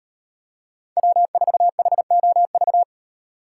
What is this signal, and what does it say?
Morse code, heard as a keyed tone, about 38 wpm, 700 Hz. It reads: W4HOV